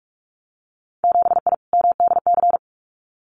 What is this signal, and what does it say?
Morse code, heard as a keyed tone, about 46 wpm, 690 Hz. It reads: 7I MDC
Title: Winter Field Day Exchanges Encore 45wpm